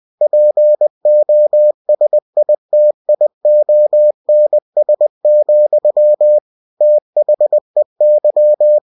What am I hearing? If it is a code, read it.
Morse code, heard as a keyed tone, about 20 wpm, 600 Hz. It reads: POSITIONS, THEY
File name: cw-books / winnie-the-pooh / winnie-20wpm-page-0095